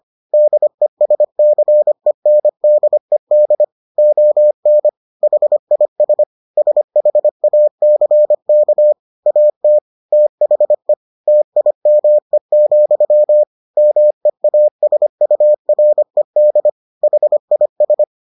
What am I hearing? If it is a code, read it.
Morse code, heard as a keyed tone, about 25 wpm, 600 Hz. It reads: DESCENDED ON HIS SHACK AT THE TIME, MEASURED HIS